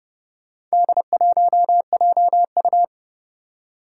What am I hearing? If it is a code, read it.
Morse code, heard as a keyed tone, about 30 wpm, 700 Hz. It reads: D1JU
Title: Four Challenging Letter-Number Combinations 30wpm